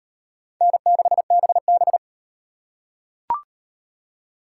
Morse code, heard as a keyed tone, about 34 wpm, 700 Hz. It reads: N6BB